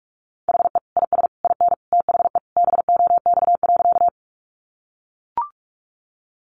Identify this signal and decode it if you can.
Morse code, heard as a keyed tone, about 44 wpm, 700 Hz. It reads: HE IS IN THE BOX.
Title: Sentences from Top 500 Words 45wpm